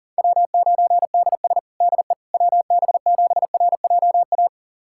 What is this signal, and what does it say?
Morse code, heard as a keyed tone, about 40 wpm, 700 Hz. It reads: W9DS DE WB7RJA
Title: USA Callsign DE Callsign 40wpm